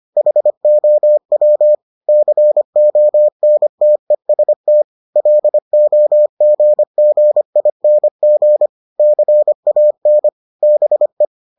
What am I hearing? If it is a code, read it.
Morse code, heard as a keyed tone, about 25 wpm, 600 Hz. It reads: HOW CONTEST LOGGING CAN BE